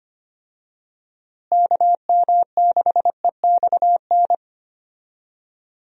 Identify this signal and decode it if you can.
Morse code, heard as a keyed tone, about 25 wpm, 700 Hz. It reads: KM6EXN
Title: USA Call Signs - 6 Characters - Repeated 2x - 25wpm